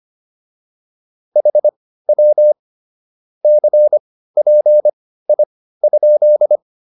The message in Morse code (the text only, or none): HW CPI?